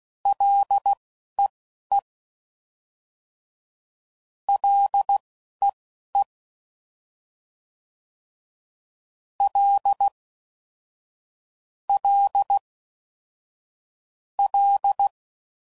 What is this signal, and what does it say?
Morse code, heard as a keyed tone, about 16 wpm, spaced as for 11 wpm, 800 Hz. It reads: LEE LEE L L L
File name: Prefixes / FLs & THs / L Prefixes